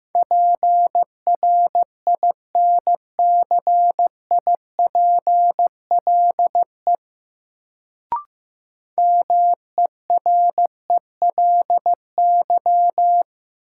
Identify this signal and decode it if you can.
Morse code, heard as a keyed tone, about 15 wpm, 700 Hz. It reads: PRINCIPLE MERELY
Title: Sentences from Top 1400 Words - New Word Drill - 15wpm